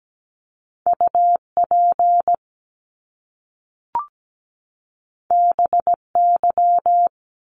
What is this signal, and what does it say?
Morse code, heard as a keyed tone, about 17 wpm, 700 Hz. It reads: UP BY
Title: Top Two Letter Combinations in Words 17wpm